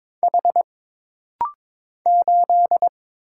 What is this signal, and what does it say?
Morse code, heard as a keyed tone, about 22 wpm, 700 Hz. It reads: H 8
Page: This practice set is composed of Letters, Numbers, common punctuations, and common prosigns.